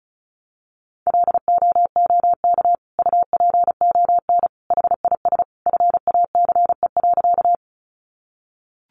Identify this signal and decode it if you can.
Morse code, heard as a keyed tone, about 34 wpm, 700 Hz. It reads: LOOK UPON HIS FACE.
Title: Sentences from Top 1100 Words 35wpm